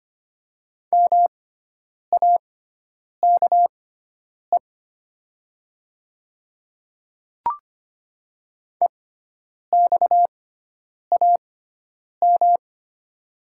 Morse code, heard as a keyed tone, about 25 wpm, spaced as for 9 wpm, 700 Hz. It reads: MAKE EXAM
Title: 6x Character Spacing - Top 300 Words - 25wpm